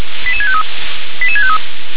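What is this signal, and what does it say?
Five-tone selective call (ZVEI), heard as quick runs of five tones, tones of about 70 ms. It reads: 88653 88653
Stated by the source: ZVEI five-tone call received by SDR.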